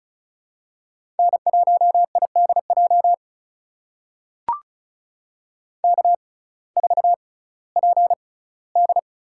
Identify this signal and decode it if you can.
Morse code, heard as a keyed tone, about 34 wpm, spaced as for 19 wpm, 700 Hz. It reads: N1IDJ K4PD
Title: USA Call Signs - 6x Character Spacing - 35wpm